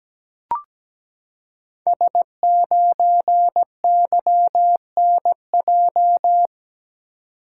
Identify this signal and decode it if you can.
Morse code, heard as a keyed tone, about 17 wpm, 700 Hz. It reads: S9YNJ
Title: Code Groups - 5 Letter:Numbers - 17wpm